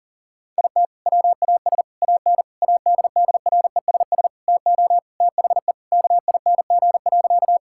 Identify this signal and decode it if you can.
Morse code, heard as a keyed tone, about 40 wpm, 700 Hz. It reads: IT WAS AN ADDRESS TO THE KING.
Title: Sentences from Top 1000 Words - Review Part 6 - 40wpm